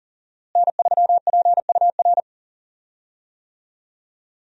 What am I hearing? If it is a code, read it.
Morse code, heard as a keyed tone, about 40 wpm, 700 Hz. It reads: N3PUR
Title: USA Call Signs - 5 Characters - Repeated 3x - 40wpm